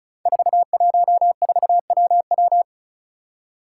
Morse code, heard as a keyed tone, about 36 wpm, 700 Hz. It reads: 414WW
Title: Five Challenging Letter-Number Combinations 35wpm